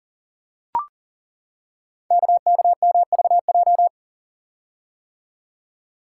Morse code, heard as a keyed tone, about 39 wpm, 700 Hz. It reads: KKMVJ
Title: Code Groups - 5 Letters - 40wpm